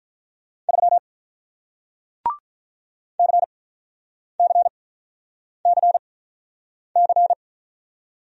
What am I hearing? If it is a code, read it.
Morse code, heard as a keyed tone, about 48 wpm, spaced as for 9 wpm, 700 Hz. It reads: 3 CCCC